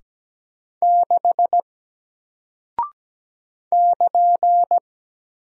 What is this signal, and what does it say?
Morse code, heard as a keyed tone, about 17 wpm, 700 Hz. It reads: 6 <KN>
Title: Single Character (Letter:Number:Punctuation:Prosign) Alt 17wpm